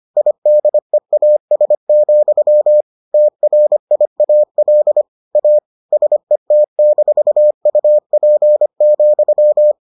Morse code, heard as a keyed tone, about 25 wpm, 600 Hz. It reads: IDEAS, TRIAL A SET-UP,